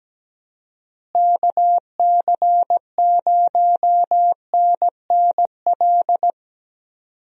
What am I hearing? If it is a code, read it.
Morse code, heard as a keyed tone, about 17 wpm, 700 Hz. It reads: KC0NNL
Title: USA Call Signs - 6 Characters - Repeated 3x - 17wpm